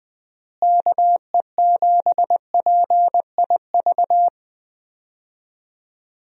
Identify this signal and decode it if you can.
Morse code, heard as a keyed tone, about 21 wpm, 700 Hz. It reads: KE7PIV
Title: USA Call Signs - 6 Characters - Repeated 3x - 20wpm